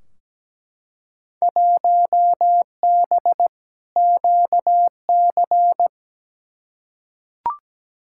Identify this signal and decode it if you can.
Morse code, heard as a keyed tone, about 17 wpm, 700 Hz. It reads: 1B QC
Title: Field Day Exchanges Encore - 17wpm